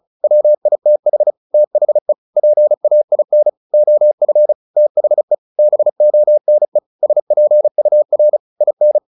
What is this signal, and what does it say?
Morse code, heard as a keyed tone, about 34 wpm, 600 Hz. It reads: WITH THE PAIN OF THE BONE SPUR IN
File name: the old man and the sea / the old man and the sea 1 35wpm